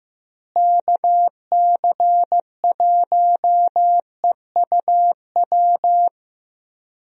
Morse code, heard as a keyed tone, about 16 wpm, 700 Hz. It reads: KC1EUW
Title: USA Call Signs - 6 Characters - Repeated 3x - 15wpm